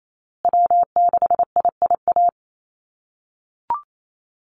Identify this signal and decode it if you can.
Morse code, heard as a keyed tone, about 28 wpm, 700 Hz. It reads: W6IIA
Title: USA Call Signs Repeated 2x 28wpm